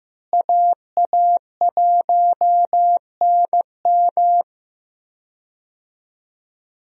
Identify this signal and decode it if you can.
Morse code, heard as a keyed tone, about 15 wpm, 700 Hz. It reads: AA1NM
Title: USA Call Signs - 5 Characters - Repeated 3x - 15wpm